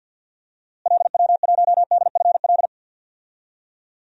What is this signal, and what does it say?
Morse code, heard as a keyed tone, about 47 wpm, 700 Hz. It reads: LW1DFL
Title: DXCC-100 Call Signs - 6 Characters - Repeated 3x - 50wpm